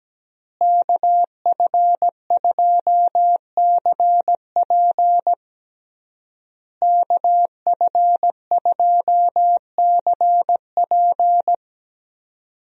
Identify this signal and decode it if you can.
Morse code, heard as a keyed tone, about 17 wpm, 700 Hz. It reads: KF2CP KF2CP
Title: USA Call Signs - 5 Characters - Repeated 3x - 17wpm